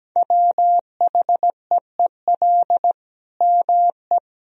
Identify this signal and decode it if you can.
Morse code, heard as a keyed tone, about 17 wpm, 700 Hz. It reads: WHEEL ME
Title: Sentences from Top 1000 Words - Review Part 3 - 17wpm